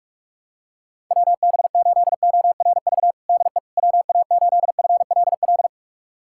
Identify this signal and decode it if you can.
Morse code, heard as a keyed tone, about 46 wpm, 700 Hz. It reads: WB8ORV DE WA8FLL
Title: USA Callsign DE Callsign - 6 Characters - 45wpm